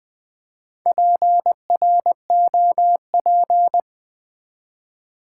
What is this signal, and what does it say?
Morse code, heard as a keyed tone, about 20 wpm, 700 Hz. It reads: PROP
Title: Build A Word - 4 Letters At A Time - 10-Letter Words - 20wpm